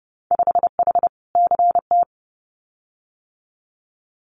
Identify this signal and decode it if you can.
Morse code, heard as a keyed tone, about 29 wpm, 700 Hz. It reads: 5H CT
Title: Winter Field Day Exchanges Encore 30wpm